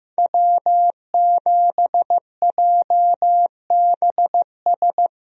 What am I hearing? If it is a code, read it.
Morse code, heard as a keyed tone, about 15 wpm, 700 Hz. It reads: W7JBS
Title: USA Call Signs - 5 Characters - Repeated 2x - 15wpm